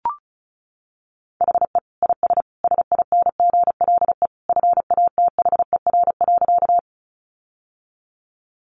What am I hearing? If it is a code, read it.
Morse code, heard as a keyed tone, about 34 wpm, 700 Hz. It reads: HE IS SINGLE FATHER.
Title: Sentences from Top 700 Words 35wpm